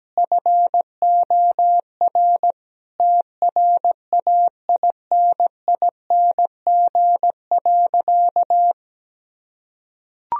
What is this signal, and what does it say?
Morse code, heard as a keyed tone, about 17 wpm, 700 Hz. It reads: FOR TRAINING.
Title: Sentences from Top 1000 Words - Review Part 4 - 17wpm